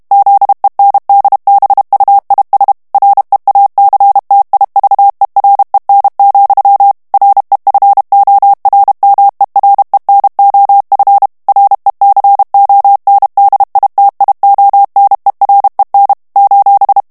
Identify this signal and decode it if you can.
Morse code, heard as a keyed tone, about 32 wpm, 800 Hz. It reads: ZENDBUIS REACTIVEREN, REFORMEREN OF RECONDITIONEREN :
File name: morse mp3 / HOLLANDAIS texte clair 2 vitesse 30 mots minute